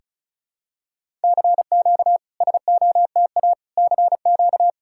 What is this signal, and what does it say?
Morse code, heard as a keyed tone, about 36 wpm, 700 Hz. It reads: CQ SOTA CQ
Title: SOTA QSO Practice 35wpm